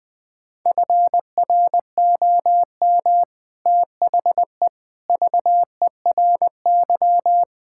Morse code, heard as a keyed tone, about 20 wpm, 700 Hz. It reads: FROM THE VERY
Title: Sentences from Top 1600 Words 20wpm